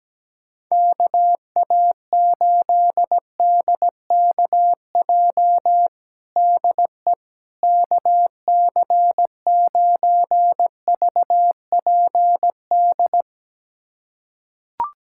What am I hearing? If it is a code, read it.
Morse code, heard as a keyed tone, about 17 wpm, 700 Hz. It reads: KA8DKJ DE KC9VPD
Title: USA Callsign DE Callsign - 6 Characters - 17wpm